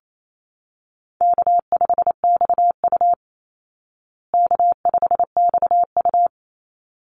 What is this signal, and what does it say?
Morse code, heard as a keyed tone, about 28 wpm, 700 Hz. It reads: K5XU K5XU
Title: Call Signs Repeated 3x - 28wpm